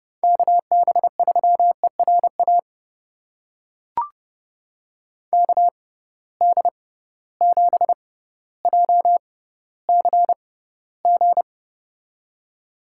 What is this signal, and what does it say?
Morse code, heard as a keyed tone, about 31 wpm, spaced as for 15 wpm, 700 Hz. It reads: KB3ERA KD7JCG
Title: USA Call Signs - 6x Character Spacing - 30wpm